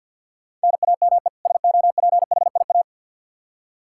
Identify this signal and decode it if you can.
Morse code, heard as a keyed tone, about 52 wpm, 700 Hz. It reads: NAME SOPHIA
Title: Practice QSO - Exchange 1 - Beginning - 50wpm